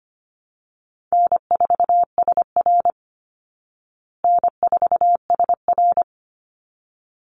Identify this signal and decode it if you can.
Morse code, heard as a keyed tone, about 25 wpm, 700 Hz. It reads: N4SR N4SR
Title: Call Signs Repeated 3x - 25wpm